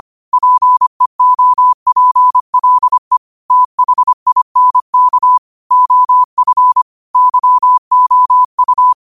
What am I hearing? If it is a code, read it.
Morse code, heard as a keyed tone, about 25 wpm, 1000 Hz. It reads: PEOPLE THINK OF YOU
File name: qotdfiles / 25wpm